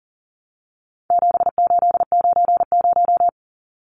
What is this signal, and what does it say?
Morse code, heard as a keyed tone, about 40 wpm, 700 Hz. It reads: 7890
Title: Numbers - 4-Digit 40wpm